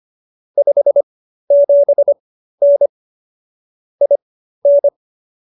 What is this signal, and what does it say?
Morse code, heard as a keyed tone, about 25 wpm, spaced as for 13 wpm, 570 Hz. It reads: 57N IN
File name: qso215b 13